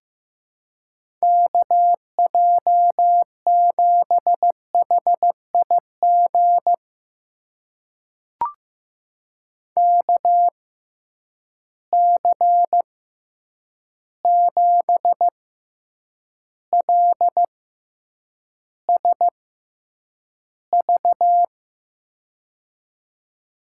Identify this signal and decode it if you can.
Morse code, heard as a keyed tone, about 15 wpm, spaced as for 8 wpm, 700 Hz. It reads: KJ7HIG KC7LSV
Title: USA Call Signs - 6 Characters - 6x Character Spacing - 15wpm